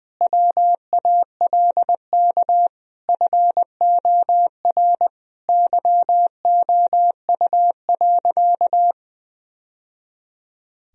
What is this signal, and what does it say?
Morse code, heard as a keyed tone, about 20 wpm, 700 Hz. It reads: WALK FOR YOU.
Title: Sentences from Top 1000 Words - Review Part 1 - 20wpm